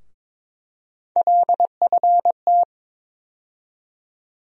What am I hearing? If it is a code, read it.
Morse code, heard as a keyed tone, about 22 wpm, 700 Hz. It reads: LFT
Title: USA Call Sign Postfixes - 3 Letters 22wpm